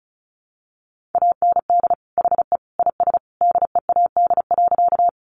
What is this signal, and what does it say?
Morse code, heard as a keyed tone, about 34 wpm, 700 Hz. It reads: AND HE IS DEAD.
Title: Sentences from Top 1500 Words 35wpm